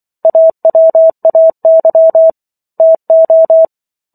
Morse code, heard as a keyed tone, about 24 wpm, 650 Hz. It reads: AWAY TO